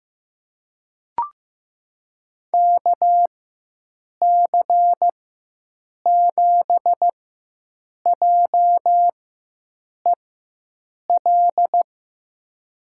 Morse code, heard as a keyed tone, about 15 wpm, spaced as for 7 wpm, 700 Hz. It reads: KC7JEL
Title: USA Call Signs - 6 Characters - 4x Character Spacing - 15wpm